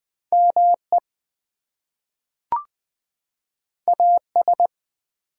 Morse code, heard as a keyed tone, about 20 wpm, 700 Hz. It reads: ME AS
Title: Two Letter Words - 20wpm